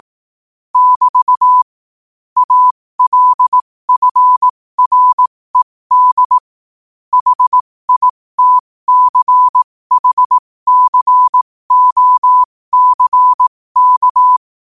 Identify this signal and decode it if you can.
Morse code, heard as a keyed tone, about 18 wpm, spaced as for 15 wpm, 1000 Hz. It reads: <BT> ALFRED HITCHCOCK
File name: qotdfiles / 15wpm